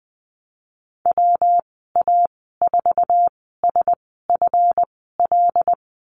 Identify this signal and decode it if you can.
Morse code, heard as a keyed tone, about 20 wpm, spaced as for 15 wpm, 700 Hz. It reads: WA4SFL
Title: 147 Lesson 24g - Letter y - Call Signs 20wpm